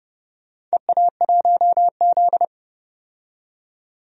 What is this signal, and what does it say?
Morse code, heard as a keyed tone, about 30 wpm, 700 Hz. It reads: EA1Z
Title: DXCC-100 Call Signs - 4 Characters - 30wpm